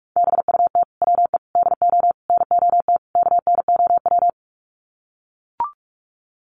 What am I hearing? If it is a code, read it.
Morse code, heard as a keyed tone, about 44 wpm, 700 Hz. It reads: BUT WE DO NOT KNOW
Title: Sets of 5 Words Encore2 45wpm